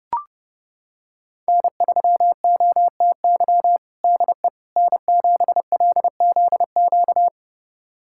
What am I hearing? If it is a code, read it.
Morse code, heard as a keyed tone, about 32 wpm, 700 Hz. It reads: N3OTY DE N7LZQ
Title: USA Callsign DE Callsign - 5 Characters - 30wpm